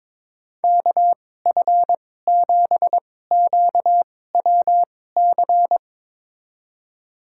Morse code, heard as a keyed tone, about 22 wpm, spaced as for 16 wpm, 700 Hz. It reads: KF7QWC
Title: USA Call Signs - 6 Characters - 2x Character Spacing - 22wpm